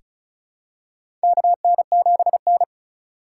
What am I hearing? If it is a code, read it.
Morse code, heard as a keyed tone, about 34 wpm, 700 Hz. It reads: KN7N
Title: USA Call Signs - 4 Characters - 2x Character Spacing - 35wpm